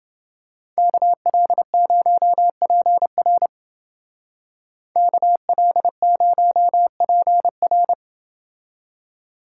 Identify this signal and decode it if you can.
Morse code, heard as a keyed tone, about 28 wpm, 700 Hz. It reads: KL0PR KL0PR